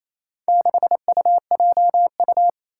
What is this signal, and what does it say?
Morse code, heard as a keyed tone, about 27 wpm, 700 Hz. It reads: 6UJU
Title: Four Challenging Letter-Number Combinations 28wpm